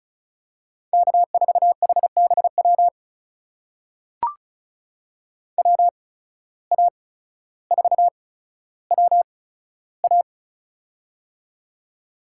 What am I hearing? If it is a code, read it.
Morse code, heard as a keyed tone, about 34 wpm, spaced as for 15 wpm, 700 Hz. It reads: K4HBW WA4WA
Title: USA Call Signs - 5 Characters - 8x Character Spacing - 35wpm